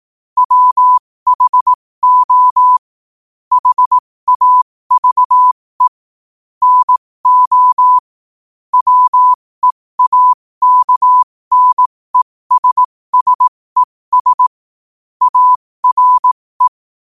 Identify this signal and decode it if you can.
Morse code, heard as a keyed tone, about 18 wpm, spaced as for 15 wpm, 1000 Hz. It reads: WHO HAVE NO WEAKNESSES ARE
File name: qotdfiles / 15wpm